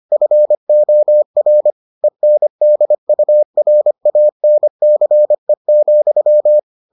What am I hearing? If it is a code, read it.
Morse code, heard as a keyed tone, about 25 wpm, 600 Hz. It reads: FOR ENDURANCE,